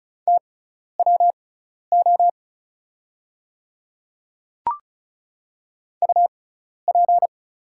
Morse code, heard as a keyed tone, about 36 wpm, spaced as for 12 wpm, 700 Hz. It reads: TWO UP
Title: 6x Character Spacing - Top 200 Words - 35wpm